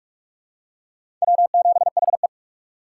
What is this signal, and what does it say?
Morse code, heard as a keyed tone, about 43 wpm, 700 Hz. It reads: W7HE
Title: USA Call Signs - 4 Characters - 4x Character Spacing - 45wpm